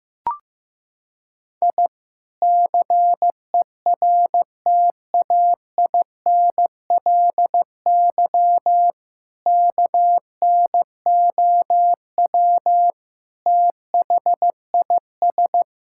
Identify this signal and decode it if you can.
Morse code, heard as a keyed tone, about 15 wpm, 700 Hz. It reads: I CERTAINLY KNOW THIS
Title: Sentences from Top 1000 Words - Review Part 6 - 15wpm